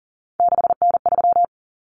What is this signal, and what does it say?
Morse code, heard as a keyed tone, about 40 wpm, 700 Hz. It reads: 6N3
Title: Common Cut Numbers - 3-Digit 40wpm